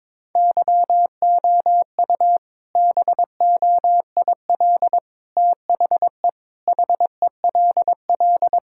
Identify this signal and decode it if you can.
Morse code, heard as a keyed tone, about 21 wpm, 700 Hz. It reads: YOU BOIL THE HELL
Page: morse-code-ninja-practice-22wpm